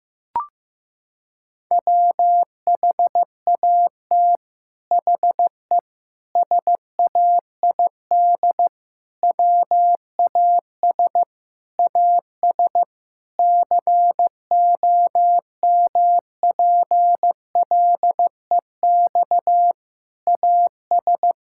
Morse code, heard as a keyed tone, about 15 wpm, 700 Hz. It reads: WHAT HE SAID WAS AS COMPLEX AS
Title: Sentences from Top 1000 Words - Review Part 6 - 15wpm